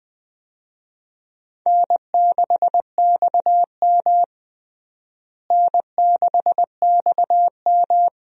Morse code, heard as a keyed tone, about 20 wpm, 700 Hz. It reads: N6XM N6XM
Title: USA Call Signs - 4 Characters - Repeated 2x - 20wpm